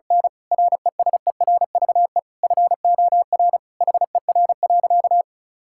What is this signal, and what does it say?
Morse code, heard as a keyed tone, about 36 wpm, 700 Hz. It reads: N RESERVE FOR HER.
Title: Sentences from Top 1000 Words - Review Part 6 - 35wpm